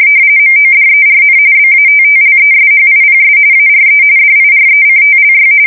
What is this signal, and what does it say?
RTTY, about 45.45 baud. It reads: MODERN BATTLESHIPS ARE DIRECTLY
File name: fsk170 / battleshipnj